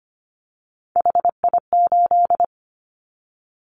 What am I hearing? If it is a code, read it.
Morse code, heard as a keyed tone, about 25 wpm, 700 Hz. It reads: HI8
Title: Callsign Prefixes for Top 100 DXCC - 25wpm